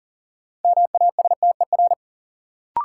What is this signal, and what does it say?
Morse code, heard as a keyed tone, about 40 wpm, 700 Hz. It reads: MASTER
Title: Sentences from Top 1300 Words - New Word Drill Alt - 40wpm